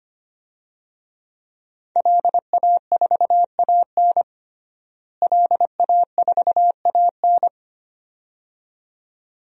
Morse code, heard as a keyed tone, about 25 wpm, 700 Hz. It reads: LA4AN LA4AN
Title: DXCC-100 Call Signs - 5 Characters - Repeated 2x - 25wpm